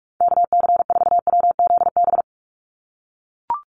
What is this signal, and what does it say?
Morse code, heard as a keyed tone, about 44 wpm, 700 Hz. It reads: KC4WZB
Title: NA SOTA Activator Callsigns 45wpm